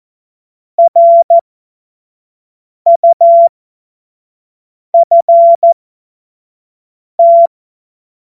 Morse code, heard as a keyed tone, about 14 wpm, spaced as for 5 wpm, 680 Hz. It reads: RUFT
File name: Morsekurs / Witze / Witz 01. März 2023 25 F70